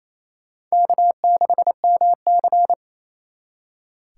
Morse code, heard as a keyed tone, about 31 wpm, 700 Hz. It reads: K6MC
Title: Speed Racing - USA Call Signs 28wpm